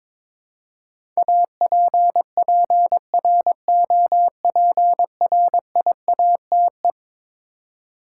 Morse code, heard as a keyed tone, about 22 wpm, 700 Hz. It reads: APPROPRIATE
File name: practice / 22wpm / Eleven Letter Words - 22wpm